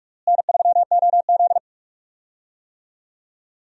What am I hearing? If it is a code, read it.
Morse code, heard as a keyed tone, about 46 wpm, 700 Hz. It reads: N3OZ
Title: USA Call Signs - 4 Characters - Repeated 2x - 45wpm